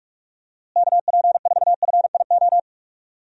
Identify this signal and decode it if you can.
Morse code, heard as a keyed tone, about 46 wpm, 700 Hz. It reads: KP4FIO
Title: USA Call Signs - 6 Characters - Repeated 3x - 45wpm